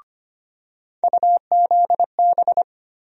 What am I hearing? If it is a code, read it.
Morse code, heard as a keyed tone, about 25 wpm, 700 Hz. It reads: UZB
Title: USA Call Sign Postfixes - 3 Letters 25wpm